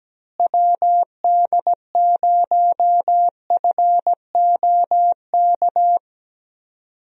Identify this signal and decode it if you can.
Morse code, heard as a keyed tone, about 17 wpm, 700 Hz. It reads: WD0FOK
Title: USA Call Signs - 6 Characters - Repeated 3x - 17wpm